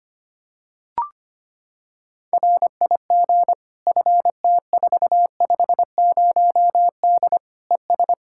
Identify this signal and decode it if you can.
Morse code, heard as a keyed tone, about 25 wpm, 700 Hz. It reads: RIG FT450D ES
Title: Practice QSO - Exchange 2 - Info - 25wpm